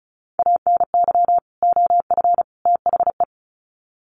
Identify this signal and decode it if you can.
Morse code, heard as a keyed tone, about 34 wpm, 700 Hz. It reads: ANY OF THE
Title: Sets of 3 Words 35wpm